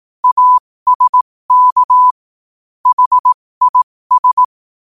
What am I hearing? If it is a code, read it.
Morse code, heard as a keyed tone, about 18 wpm, spaced as for 15 wpm, 1000 Hz. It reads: ASK HIS
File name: qotdfiles / 15wpm